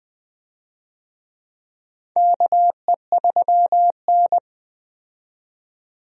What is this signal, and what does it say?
Morse code, heard as a keyed tone, about 20 wpm, 700 Hz. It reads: KE3N
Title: USA Call Signs - 4 Characters - 20wpm